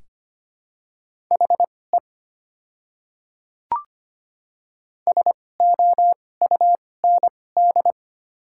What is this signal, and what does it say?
Morse code, heard as a keyed tone, about 25 wpm, spaced as for 18 wpm, 690 Hz. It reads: HE SOUND
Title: 2x Character Spacing - Top 100 Words - 25wpm